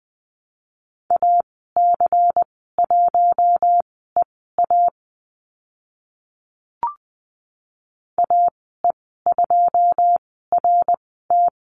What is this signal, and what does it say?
Morse code, heard as a keyed tone, about 20 wpm, spaced as for 15 wpm, 700 Hz. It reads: AC1EA AE2RT
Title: 154 Lesson 25g - Letter p - Call Signs 20wpm